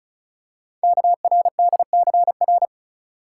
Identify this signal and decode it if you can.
Morse code, heard as a keyed tone, about 36 wpm, 690 Hz. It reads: KRDCR
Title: Code Groups - 5 Letters - 35wpm